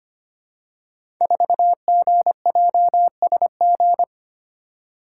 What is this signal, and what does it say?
Morse code, heard as a keyed tone, about 25 wpm, 700 Hz. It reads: 4GJSG
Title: Code Groups - 5 Letter:Numbers - 25wpm